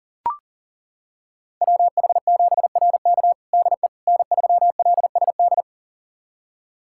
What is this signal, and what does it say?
Morse code, heard as a keyed tone, about 40 wpm, 700 Hz. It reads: WH7RK DE N3LSD
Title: USA Callsign DE Callsign - 5 Characters - 40wpm